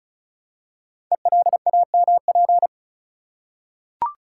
Morse code, heard as a keyed tone, about 36 wpm, 690 Hz. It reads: ELAMP
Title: Code Groups - 5 Letters - 35wpm